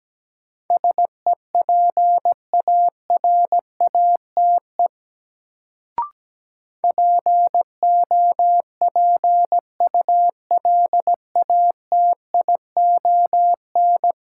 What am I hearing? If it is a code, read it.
Morse code, heard as a keyed tone, about 17 wpm, 700 Hz. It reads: SEPARATE POPULATION
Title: Sentences from Top 1000 Words - Review Part 7 - New Word Drill - 17wpm